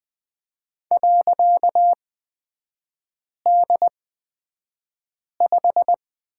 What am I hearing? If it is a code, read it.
Morse code, heard as a keyed tone, about 20 wpm, spaced as for 5 wpm, 700 Hz. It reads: .D5
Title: 255 Lesson 41e - Period - Three Characters 20wpm